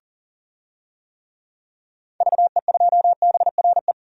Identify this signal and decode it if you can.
Morse code, heard as a keyed tone, about 40 wpm, 700 Hz. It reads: VE2BRE
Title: DXCC-100 Call Signs - 6 Characters - Repeated 3x - 40wpm